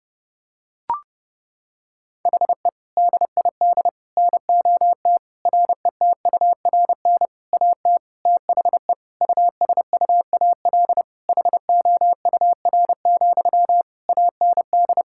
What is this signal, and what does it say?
Morse code, heard as a keyed tone, about 32 wpm, 700 Hz. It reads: HE DID NOT RETURN AT THE USUAL HOUR, AND
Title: Sentences from Top 1600 Words 30wpm